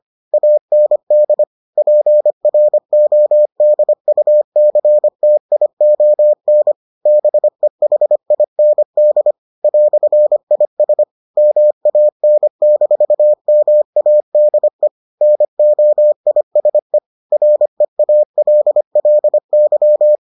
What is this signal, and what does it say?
Morse code, heard as a keyed tone, about 25 wpm, 600 Hz. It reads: AND PRODUCTION BEHIND "IS MAN-MADE NOISE REALLY